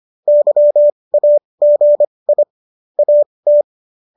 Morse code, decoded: YAGI AT

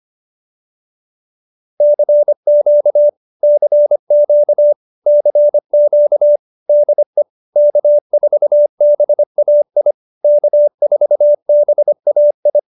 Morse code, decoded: CQ CQ CQ DE K4BAI K4BAI

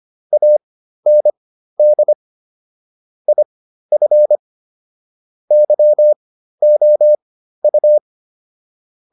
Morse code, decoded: AND IF YOU

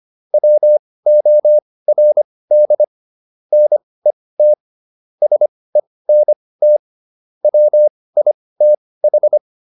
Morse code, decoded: WORD NET SENT WITH

